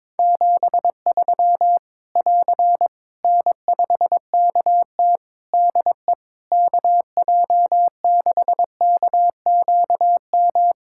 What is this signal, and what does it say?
Morse code, heard as a keyed tone, about 22 wpm, 700 Hz. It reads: 73 <AR> N5KT DE KJ6KQM
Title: Practice QSO - Exchange 4 - Ending - 22wpm